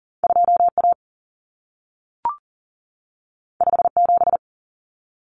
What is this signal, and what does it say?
Morse code, heard as a keyed tone, about 40 wpm, 700 Hz. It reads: 2A 57
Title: Common Cut Numbers - 2-Digit 40wpm